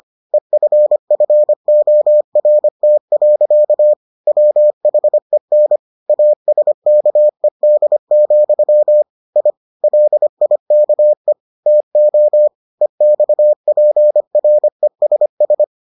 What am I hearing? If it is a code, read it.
Morse code, heard as a keyed tone, about 25 wpm, 600 Hz. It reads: EFFORT. WHEN ASKED, I LIKE TO EXPRESS